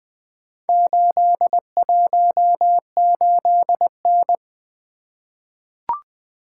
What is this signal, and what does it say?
Morse code, heard as a keyed tone, about 20 wpm, 700 Hz. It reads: 818N
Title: Four Letter-Number Combinations - 20wpm